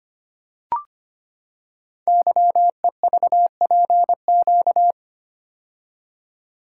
Morse code, heard as a keyed tone, about 25 wpm, 700 Hz. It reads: YEVPQ
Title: Code Groups - 5 Letters - 25wpm